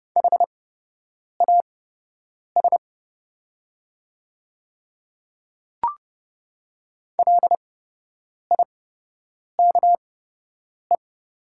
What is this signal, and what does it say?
Morse code, heard as a keyed tone, about 30 wpm, spaced as for 8 wpm, 700 Hz. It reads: HAS LIKE
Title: 8x Character Spacing - Top 200 Words - 30wpm